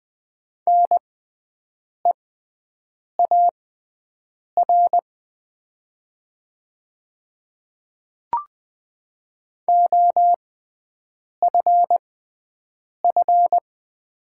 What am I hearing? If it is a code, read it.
Morse code, heard as a keyed tone, about 21 wpm, spaced as for 7 wpm, 700 Hz. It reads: NEAR OFF
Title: 6x Character Spacing - Top 500 Words - 20wpm